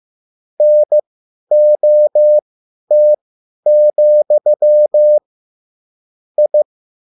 Morse code, heard as a keyed tone, about 15 wpm, spaced as for 10 wpm, 600 Hz. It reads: NOT, I
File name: QSL / CW lesson